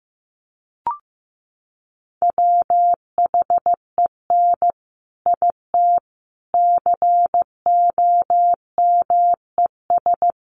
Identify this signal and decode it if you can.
Morse code, heard as a keyed tone, about 15 wpm, 700 Hz. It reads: WHEN IT COMES
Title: Sets of 5 Words 15wpm